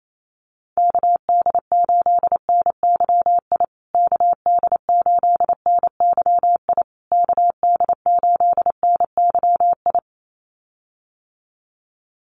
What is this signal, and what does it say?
Morse code, heard as a keyed tone, about 28 wpm, 700 Hz. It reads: KD8NYI KD8NYI KD8NYI
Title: USA Call Signs Repeated 3x 28wpm